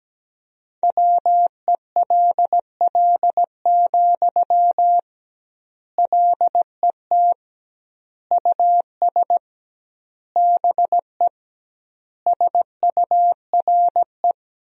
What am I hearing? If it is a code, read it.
Morse code, heard as a keyed tone, about 17 wpm, 700 Hz. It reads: WELL, LET US BE SURE
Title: 2x Word-Spacing - Sentences from Top 300 Words - 17wpm